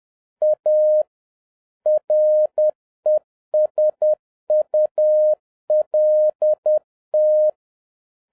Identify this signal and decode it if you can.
Morse code, heard as a keyed tone, about 10 wpm, 610 Hz. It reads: A RESULT